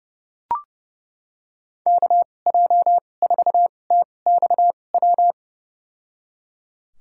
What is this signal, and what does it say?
Morse code, heard as a keyed tone, about 32 wpm, spaced as for 22 wpm, 700 Hz. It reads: KJ4TXW
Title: USA Call Signs - 6 Characters - 2x Character Spacing - 30wpm